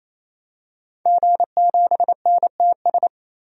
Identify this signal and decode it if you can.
Morse code, heard as a keyed tone, about 28 wpm, 700 Hz. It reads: G7NTS
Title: DXCC-100 Call Signs - 5 Characters - 28wpm